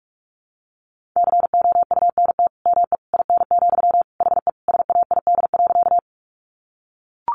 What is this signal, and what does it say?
Morse code, heard as a keyed tone, about 44 wpm, 700 Hz. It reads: COUNT ME IN, HE SAID.
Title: Sentences from Top 1500 Words 45wpm